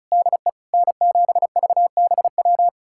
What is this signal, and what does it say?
Morse code, heard as a keyed tone, about 34 wpm, 700 Hz. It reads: DE N7VBW
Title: USA Callsign DE Callsign 35wpm